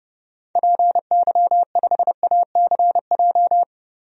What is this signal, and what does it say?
Morse code, heard as a keyed tone, about 30 wpm, 700 Hz. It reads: PY5ACJ
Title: DXCC-100 Call Signs - 6 Characters - 30wpm